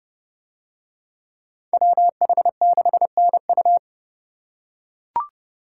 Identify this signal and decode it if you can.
Morse code, heard as a keyed tone, about 29 wpm, 700 Hz. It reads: WH6NU